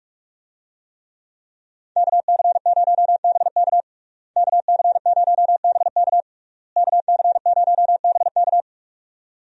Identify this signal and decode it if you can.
Morse code, heard as a keyed tone, about 46 wpm, 700 Hz. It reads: KC0BK KC0BK KC0BK
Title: USA Call Signs - 5 Characters - Repeated 3x - 45wpm